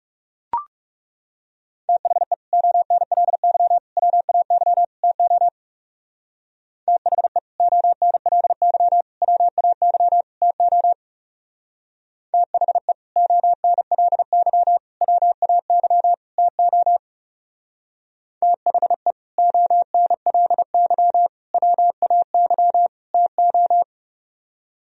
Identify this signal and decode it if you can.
Morse code, heard as a keyed tone, about 36 wpm, 700 Hz. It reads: THE ONLY WAY TO THE ONLY WAY TO THE ONLY WAY TO THE ONLY WAY TO
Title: Speed-Racing - Sets of 4 Words 30wpm